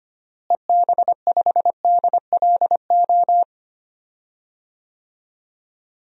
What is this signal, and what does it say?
Morse code, heard as a keyed tone, about 25 wpm, 700 Hz. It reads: EB5DLO
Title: DXCC-100 Call Signs - 6 Characters - Repeated 2x - 25wpm